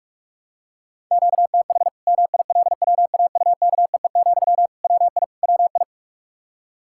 Morse code, heard as a keyed tone, about 41 wpm, 700 Hz. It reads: QTH MILWAUKEE, WI WI
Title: K1USN SST QSO Practice 45wpm